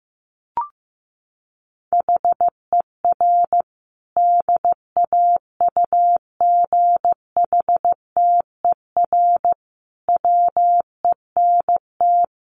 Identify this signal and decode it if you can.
Morse code, heard as a keyed tone, about 15 wpm, 700 Hz. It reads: HER DAUGHTER WENT
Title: Sentences from Top 1500 Words 15wpm